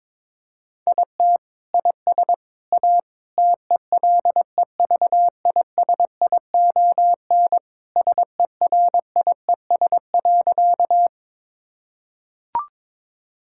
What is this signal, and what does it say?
Morse code, heard as a keyed tone, about 22 wpm, 700 Hz. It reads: IT IS A TELEVISION SERIES.
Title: Sentences from Top 1000 Words - Review Part 5 - 22wpm